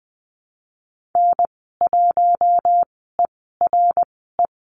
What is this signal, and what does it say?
Morse code, heard as a keyed tone, about 20 wpm, spaced as for 15 wpm, 700 Hz. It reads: N1ERE
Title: 082 Lesson 14g - Number 2 - Call Signs 20wpm